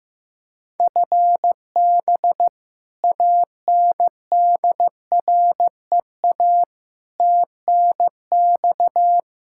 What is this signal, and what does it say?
Morse code, heard as a keyed tone, about 15 wpm, 700 Hz. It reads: FB ANDREA TNX
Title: Practice QSO - Exchange 3 - More Info - 15wpm